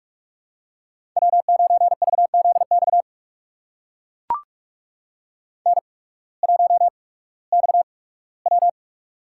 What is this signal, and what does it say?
Morse code, heard as a keyed tone, about 44 wpm, spaced as for 21 wpm, 700 Hz. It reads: W9VZX N1XW
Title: USA Call Signs - 8x Character Spacing - 45wpm